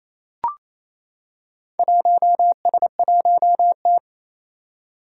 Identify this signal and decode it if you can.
Morse code, heard as a keyed tone, about 27 wpm, 700 Hz. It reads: 1S1T
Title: Four Challenging Letter-Number Combinations 28wpm